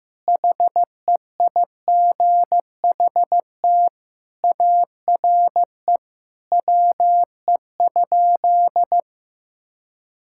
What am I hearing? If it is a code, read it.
Morse code, heard as a keyed tone, about 14 wpm, 700 Hz. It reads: HEIGHT ARE WE?
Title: Sentences from Top 1600 Words 15wpm